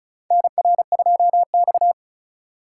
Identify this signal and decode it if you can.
Morse code, heard as a keyed tone, about 36 wpm, 700 Hz. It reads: NR2X
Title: USA Call Signs - 4 Characters - 3x Character Spacing - 35wpm